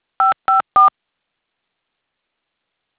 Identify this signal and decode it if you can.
DTMF, heard as telephone touch tones, tones of about 140 ms. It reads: 554
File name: things / adit fxo tellme